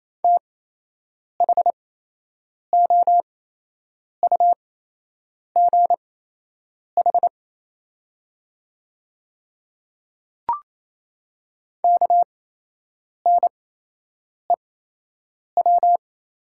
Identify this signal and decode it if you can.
Morse code, heard as a keyed tone, about 28 wpm, spaced as for 8 wpm, 700 Hz. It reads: THOUGH KNEW
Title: 8x Character Spacing - Top 500 Words - 28wpm